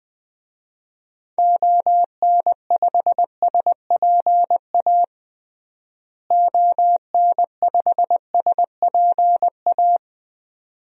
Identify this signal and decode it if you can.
Morse code, heard as a keyed tone, about 21 wpm, 700 Hz. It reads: ON5SPA ON5SPA
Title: DXCC-100 Call Signs - 6 Characters - Repeated 2x - 20wpm